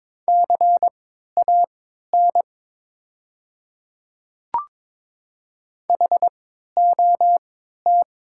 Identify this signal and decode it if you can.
Morse code, heard as a keyed tone, about 21 wpm, spaced as for 12 wpm, 700 Hz. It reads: CAN HOT
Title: 3x Character Spacing - Top 200 Words - 22wpm